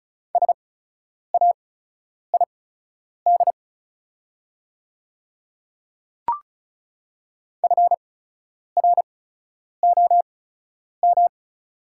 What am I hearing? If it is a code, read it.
Morse code, heard as a keyed tone, about 36 wpm, spaced as for 10 wpm, 700 Hz. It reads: SAID FROM
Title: 8x Character Spacing - Top 100 Words - 35wpm